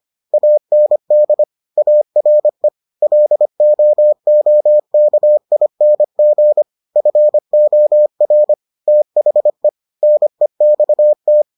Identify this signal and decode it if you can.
Morse code, heard as a keyed tone, about 25 wpm, 600 Hz. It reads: AND ARE LOOKING FOR THE NEXT